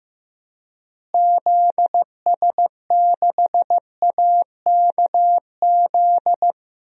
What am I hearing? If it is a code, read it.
Morse code, heard as a keyed tone, about 15 wpm, 700 Hz. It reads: ZS6AKZ
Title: DXCC-100 Call Signs - 6 Characters - Repeated 3x - 15wpm